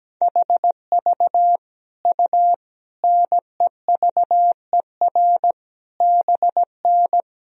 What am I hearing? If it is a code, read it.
Morse code, heard as a keyed tone, about 17 wpm, 700 Hz. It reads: HV U NEVER BN
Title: Sentences from Top 300 Words and Abbreviations 17wpm